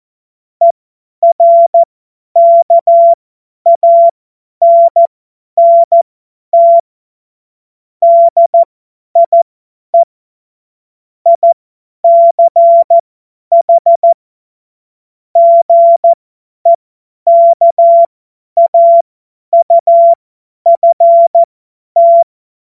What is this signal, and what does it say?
Morse code, heard as a keyed tone, about 14 wpm, spaced as for 10 wpm, 680 Hz. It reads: ERKANNT DIE ICH GEKAUFT